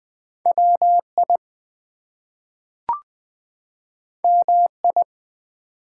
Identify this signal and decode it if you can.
Morse code, heard as a keyed tone, about 20 wpm, 700 Hz. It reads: WI MI
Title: ARRL Sections v2 - 20wpm